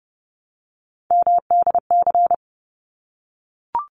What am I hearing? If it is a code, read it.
Morse code, heard as a keyed tone, about 30 wpm, 700 Hz. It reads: MDC
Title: ARRL Sections 30wpm